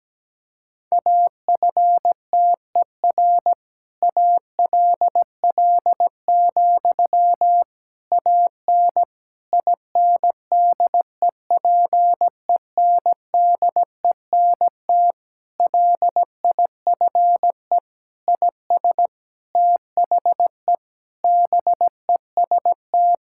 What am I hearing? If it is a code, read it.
Morse code, heard as a keyed tone, about 17 wpm, 700 Hz. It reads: AFTER ALL, AN INDEPENDENT LIFE IS THE BEST
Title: Sentences from Top 1000 Words - Review Part 6 - 17wpm